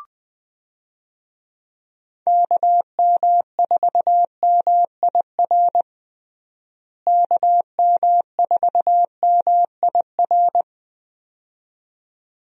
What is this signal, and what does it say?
Morse code, heard as a keyed tone, about 21 wpm, 700 Hz. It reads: KM4MIR KM4MIR
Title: USA Call Signs - 6 Characters - Repeated 2x - 20wpm